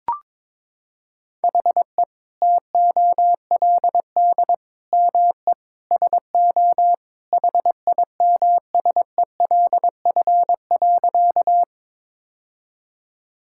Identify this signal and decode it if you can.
Morse code, heard as a keyed tone, about 22 wpm, 700 Hz. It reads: HE TOLD ME SO HIMSELF.
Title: Sentences from Top 1600 Words 22wpm